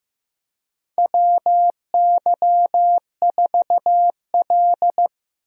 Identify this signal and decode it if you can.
Morse code, heard as a keyed tone, about 15 wpm, 700 Hz. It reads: WY4L
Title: USA Call Signs - 4 Characters - 15wpm